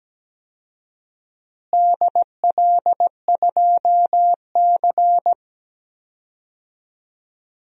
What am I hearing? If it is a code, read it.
Morse code, heard as a keyed tone, about 17 wpm, 700 Hz. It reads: DL2C
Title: DXCC-100 Call Signs - 4 Characters - Repeated 2x - 17wpm